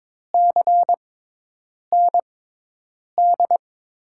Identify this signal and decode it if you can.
Morse code, heard as a keyed tone, about 22 wpm, spaced as for 8 wpm, 700 Hz. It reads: CND
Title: USA Call Signs - 6 Characters - 6x Character Spacing - 22wpm